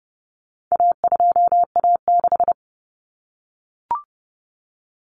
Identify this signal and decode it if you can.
Morse code, heard as a keyed tone, about 30 wpm, 700 Hz. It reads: A2A6
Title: Common Cut Numbers - 4-Digit 30wpm